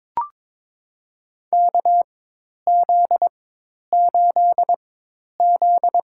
Morse code, heard as a keyed tone, about 21 wpm, spaced as for 10 wpm, 700 Hz. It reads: KZ8Z
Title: USA Call Signs - 4 Characters - 4x Character Spacing - 22wpm